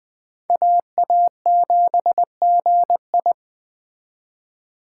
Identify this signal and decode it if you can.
Morse code, heard as a keyed tone, about 20 wpm, 700 Hz. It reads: AA7GI
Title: Speed Racing - USA Call Signs 17wpm